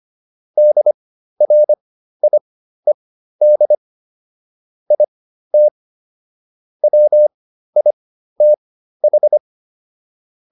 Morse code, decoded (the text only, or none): DRIED IT WITH